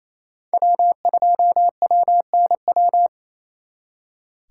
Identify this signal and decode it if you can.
Morse code, heard as a keyed tone, about 34 wpm, 700 Hz. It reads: W2WNW
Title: Speed Racing - USA 1x3 Call Signs 28wpm